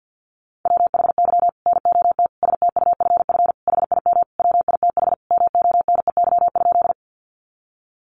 Morse code, heard as a keyed tone, about 50 wpm, 700 Hz. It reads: WHY NOT STUFF HIM WITH MONEY?
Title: Sentences from Top 1400 Words 50wpm